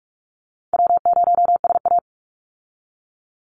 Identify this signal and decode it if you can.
Morse code, heard as a keyed tone, about 46 wpm, 700 Hz. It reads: W0SA
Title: Easy Call Signs - 45wpm